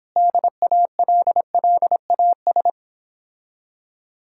Morse code, heard as a keyed tone, about 26 wpm, 700 Hz. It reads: DALLAS